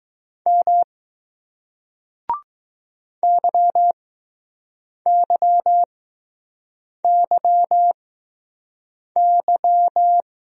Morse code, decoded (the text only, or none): M YYYY